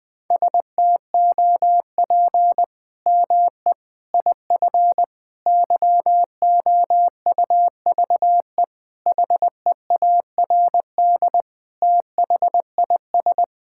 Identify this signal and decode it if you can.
Morse code, heard as a keyed tone, about 20 wpm, 700 Hz. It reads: STOP ME IF YOUVE HEARD THIS
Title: Sets of 8 Words - 20wpm